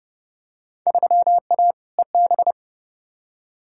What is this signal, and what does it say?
Morse code, heard as a keyed tone, about 30 wpm, 700 Hz. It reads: 3A EB
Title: Field Day Exchanges Encore - 30wpm